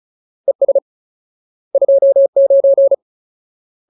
Morse code, heard as a keyed tone, about 34 wpm, 550 Hz. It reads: ES 29
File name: Ultimate Word List / 35WPM Ultimate